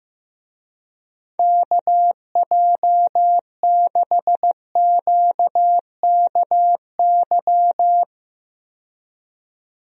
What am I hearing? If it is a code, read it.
Morse code, heard as a keyed tone, about 15 wpm, 700 Hz. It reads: KJ6QKY
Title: USA Call Signs - 6 Characters - 15wpm